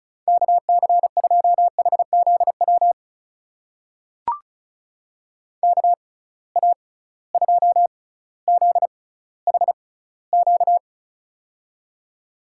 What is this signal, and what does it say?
Morse code, heard as a keyed tone, about 34 wpm, spaced as for 18 wpm, 700 Hz. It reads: KC2HZW KA2ZHQ
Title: USA Call Signs - 6 Characters - 6x Character Spacing - 35wpm